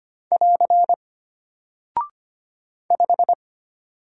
Morse code, heard as a keyed tone, about 25 wpm, 700 Hz. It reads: <AR> 5
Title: Single Character (Letter:Number:Punctuation:Prosign) Alt 25wpm